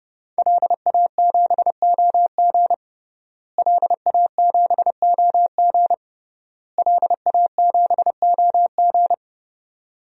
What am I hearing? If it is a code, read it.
Morse code, heard as a keyed tone, about 32 wpm, 700 Hz. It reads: LA7OG LA7OG LA7OG
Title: DXCC-100 Call Signs - 5 Characters - Repeated 3x - 30wpm